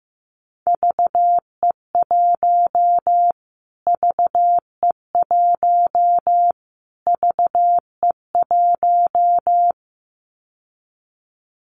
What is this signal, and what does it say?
Morse code, heard as a keyed tone, about 15 wpm, 700 Hz. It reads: VE1 VE1 VE1
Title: Callsign Prefixes for Top 25 DXCC - Repeated 3x - 15wpm